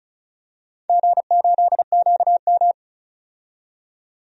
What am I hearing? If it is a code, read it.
Morse code, heard as a keyed tone, about 36 wpm, 690 Hz. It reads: G8QM
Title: DXCC-100 Call Signs - 4 Characters - 35wpm